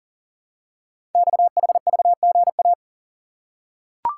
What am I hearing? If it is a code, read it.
Morse code, heard as a keyed tone, about 42 wpm, 690 Hz. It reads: XHVGA
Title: Code Groups - 5 Letters - 40wpm